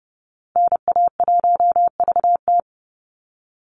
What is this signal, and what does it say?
Morse code, heard as a keyed tone, about 30 wpm, 700 Hz. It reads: NA1VT
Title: Call Signs Repeated 2x - 30wpm